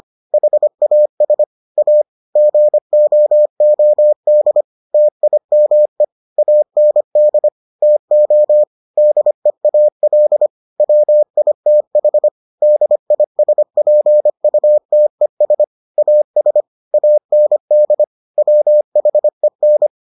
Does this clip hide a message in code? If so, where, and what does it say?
Yes: Morse code from 0.3 to 19.9 s, heard as a keyed tone, about 25 wpm, 600 Hz. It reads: HAS A GOOD TIME AND TO DEAL WITH DISPUTES AS AND WHEN